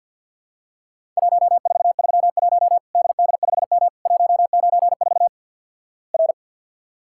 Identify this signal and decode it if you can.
Morse code, heard as a keyed tone, about 48 wpm, 700 Hz. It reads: 1431 DD5M 194 R
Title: Worked All Europe DX Contest - QTC Exchange with Repeats - 50wpm